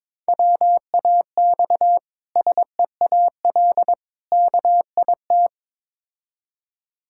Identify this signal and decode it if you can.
Morse code, heard as a keyed tone, about 22 wpm, 690 Hz. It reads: WAX SEAL KIT
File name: Difficult Words with Hints - 22wpm